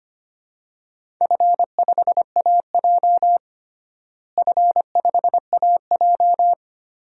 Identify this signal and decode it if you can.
Morse code, heard as a keyed tone, about 25 wpm, 700 Hz. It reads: F5AJ F5AJ
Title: DXCC-100 Call Signs - 4 Characters - Repeated 3x - 25wpm